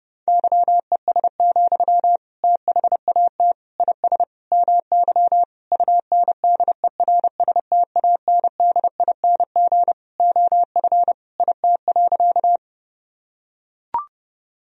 Morse code, heard as a keyed tone, about 31 wpm, 700 Hz. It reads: YES, THAT IS MY UNDERSTANDING OF IT.
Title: Sentences from Top 1000 Words - Review Part 3 - 30wpm